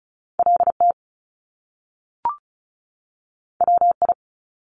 Morse code, heard as a keyed tone, about 36 wpm, 700 Hz. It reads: LT WI
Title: Top Two Letter Combinations in Words 35wpm